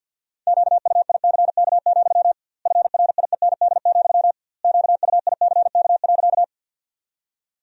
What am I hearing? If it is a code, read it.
Morse code, heard as a keyed tone, about 50 wpm, 700 Hz. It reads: QUICK, FRIEND, QUICK.
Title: 2x Word-Spacing - Sentences from Top 500 Words - 50wpm